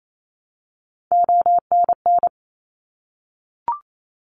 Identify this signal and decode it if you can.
Morse code, heard as a keyed tone, about 28 wpm, 700 Hz. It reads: ONN
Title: ARRL Sections 28wpm